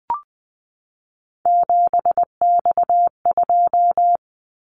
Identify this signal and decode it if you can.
Morse code, heard as a keyed tone, about 20 wpm, 700 Hz. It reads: 7X2
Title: Callsign Prefixes for Top 100 DXCC - 20wpm